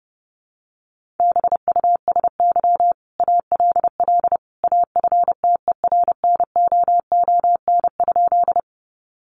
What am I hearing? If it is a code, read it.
Morse code, heard as a keyed tone, about 28 wpm, 700 Hz. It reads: BUSY ALL AFTERNOON?
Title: Sentences from Top 1500 Words 30wpm